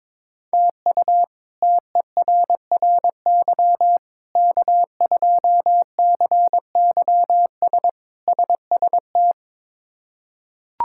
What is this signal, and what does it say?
Morse code, heard as a keyed tone, about 22 wpm, 700 Hz. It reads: TU TERRY K2CYS SST
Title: K1USN SST QSO Practice - 22wpm